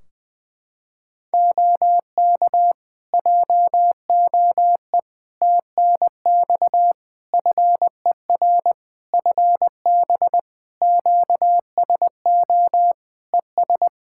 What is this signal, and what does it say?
Morse code, heard as a keyed tone, about 20 wpm, 700 Hz. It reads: OK JOE TNX FER FB QSO ES
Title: K1USN SST QSO Practice 20wpm